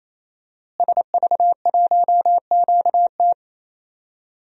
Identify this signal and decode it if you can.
Morse code, heard as a keyed tone, about 28 wpm, 700 Hz. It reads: SV1QT
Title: DXCC-100 Call Signs - 5 Characters - Repeated 3x - 28wpm